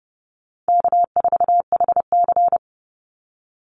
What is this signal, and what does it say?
Morse code, heard as a keyed tone, about 30 wpm, 700 Hz. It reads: K4HC
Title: Easy Call Signs Repeated 3x - 30wpm